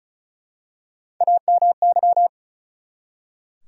Morse code, heard as a keyed tone, about 36 wpm, 700 Hz. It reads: AMY
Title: USA Common Names - Max 4 Letters - 35wpm